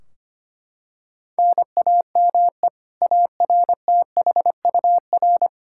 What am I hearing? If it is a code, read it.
Morse code, heard as a keyed tone, about 25 wpm, 700 Hz. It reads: NAME ARTHUR
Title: Practice QSO - Exchange 1 - Beginning - 25wpm